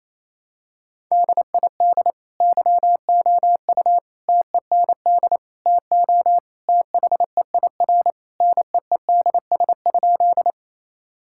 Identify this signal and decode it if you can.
Morse code, heard as a keyed tone, about 28 wpm, 700 Hz. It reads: DID YOU TEND TO THEIR NEEDS?
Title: Sentences from Top 1000 Words - Review Part 5 - 28wpm